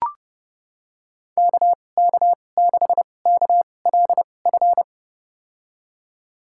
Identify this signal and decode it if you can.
Morse code, heard as a keyed tone, about 30 wpm, spaced as for 22 wpm, 700 Hz. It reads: KK6KLF
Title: USA Call Signs - 6 Characters - 2x Character Spacing - 30wpm